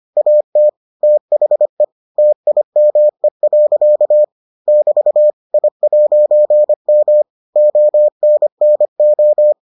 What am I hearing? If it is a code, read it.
Morse code, heard as a keyed tone, about 25 wpm, 600 Hz. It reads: AT THE TIME. <BT> I'M ONNO